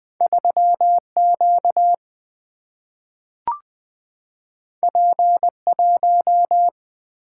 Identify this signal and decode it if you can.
Morse code, heard as a keyed tone, about 20 wpm, 700 Hz. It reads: 3Q P1
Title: Single-Letter Single-Number 20wpm